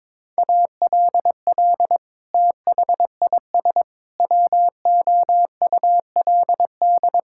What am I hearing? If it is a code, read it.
Morse code, heard as a keyed tone, about 22 wpm, 700 Hz. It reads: ALL THIS WOULD
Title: Sentences from Top 1000 Words - Review Part 6 - 22wpm